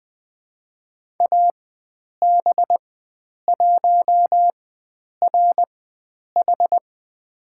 Morse code, heard as a keyed tone, about 20 wpm, spaced as for 9 wpm, 700 Hz. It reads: AB1RH
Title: USA Call Signs - 5 Characters - 4x Character Spacing - 20wpm